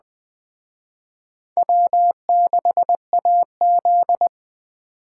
Morse code, heard as a keyed tone, about 20 wpm, 700 Hz. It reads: W6AZ
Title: Speed Racing - USA 1x2 Call Signs 17wpm